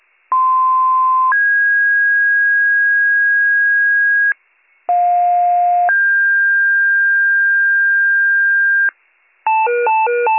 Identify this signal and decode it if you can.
Two-tone page, heard as a one-second tone, then a three-second tone, about 1040 then 1650 Hz, about 710 then 1600 Hz. It is heard from 0.3 to 8.9 s.